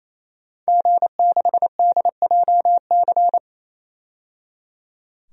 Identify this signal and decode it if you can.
Morse code, heard as a keyed tone, about 28 wpm, 690 Hz. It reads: G6DJC
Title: DXCC-100 Call Signs - 5 Characters - 28wpm